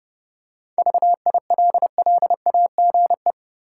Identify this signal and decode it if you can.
Morse code, heard as a keyed tone, about 30 wpm, 700 Hz. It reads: VILLAGE
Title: Seven Letter Words - 30wpm